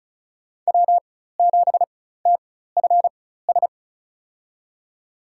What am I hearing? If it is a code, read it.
Morse code, heard as a keyed tone, about 34 wpm, spaced as for 16 wpm, 700 Hz. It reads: W7TFS
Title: USA Call Signs - 5 Characters - 4x Character Spacing - 35wpm